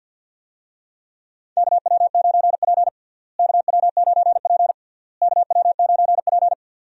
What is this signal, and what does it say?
Morse code, heard as a keyed tone, about 51 wpm, 700 Hz. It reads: KW9P KW9P KW9P
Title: USA Call Signs - 4 Characters - Repeated 3x - 50wpm